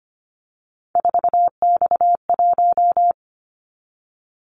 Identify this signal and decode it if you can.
Morse code, heard as a keyed tone, about 25 wpm, 700 Hz. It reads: 4X1
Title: Callsign Prefixes for Top 50 DXCC - 25wpm